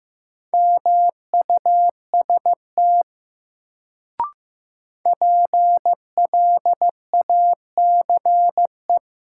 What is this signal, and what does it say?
Morse code, heard as a keyed tone, about 15 wpm, 700 Hz. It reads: MUST PLACE
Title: Sentences from Top 200 Words - New Word Drill - 15wpm